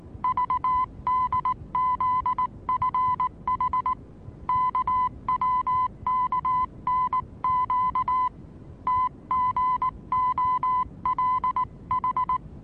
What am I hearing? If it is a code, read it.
Morse code, heard as a keyed tone, about 19 wpm, 1010 Hz. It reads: VDZFH KWKNQ TGOLH
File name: repos / Morsekurs / DJ4UF Morsen26